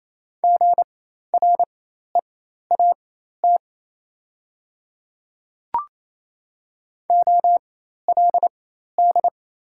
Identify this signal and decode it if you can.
Morse code, heard as a keyed tone, about 28 wpm, spaced as for 13 wpm, 690 Hz. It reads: GREAT OLD
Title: 4x Character Spacing - Top 200 Words - 28wpm